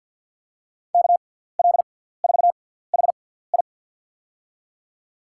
Morse code, heard as a keyed tone, about 49 wpm, spaced as for 17 wpm, 700 Hz. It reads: KL4HI
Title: USA Call Signs - 6x Character Spacing - 50wpm